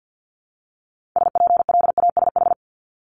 Morse code, heard as a keyed tone, about 50 wpm, 700 Hz. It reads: SPLASH
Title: Multi-letter Phonemes 50wpm